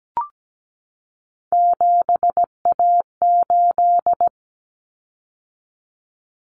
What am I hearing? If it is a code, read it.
Morse code, heard as a keyed tone, about 17 wpm, 700 Hz. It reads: 7A8
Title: Common Cut Numbers - 3-Digit 17wpm